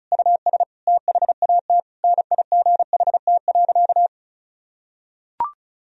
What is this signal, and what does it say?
Morse code, heard as a keyed tone, about 34 wpm, 700 Hz. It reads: US THAT NIGHT.
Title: Sentences from Top 1000 Words - Review Part 2 - 35wpm